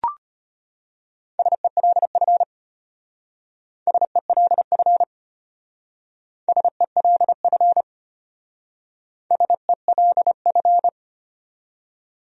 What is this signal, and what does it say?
Morse code, heard as a keyed tone, about 32 wpm, 700 Hz. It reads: SELF SELF SELF SELF